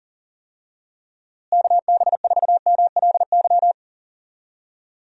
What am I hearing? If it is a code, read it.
Morse code, heard as a keyed tone, about 40 wpm, 700 Hz. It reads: KB4MLY
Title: USA Call Signs - 6 Characters - Repeated 2x - 40wpm